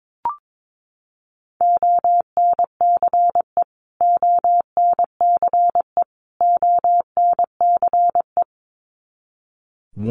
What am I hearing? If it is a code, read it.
Morse code, heard as a keyed tone, about 22 wpm, 700 Hz. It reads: ONCE ONCE ONCE
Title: Top 300 Words Repeated 22wpm